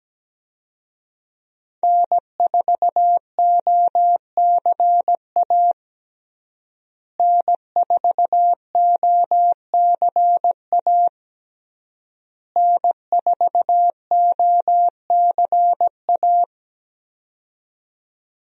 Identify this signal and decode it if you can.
Morse code, heard as a keyed tone, about 17 wpm, 700 Hz. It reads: N4OCA N4OCA N4OCA
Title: USA Call Signs - 5 Characters - Repeated 3x - 17wpm